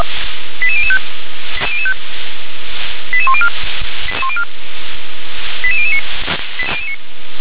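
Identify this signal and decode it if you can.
Five-tone selective call (ZVEI), heard as quick runs of five tones, tones of about 70 ms. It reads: 88005 88104 88009
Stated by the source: ZVEI five-tone selective call received on SDR.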